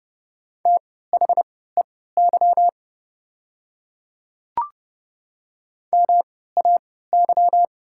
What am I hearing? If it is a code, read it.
Morse code, heard as a keyed tone, about 31 wpm, spaced as for 17 wpm, 700 Hz. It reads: THEY MAY
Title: 3x Character Spacing - Top 100 Words - 30wpm